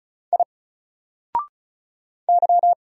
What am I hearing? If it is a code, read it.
Morse code, heard as a keyed tone, about 36 wpm, 700 Hz. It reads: I Y
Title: Single Character (Letter:Number:Punctuation:Prosign) Alt 35wpm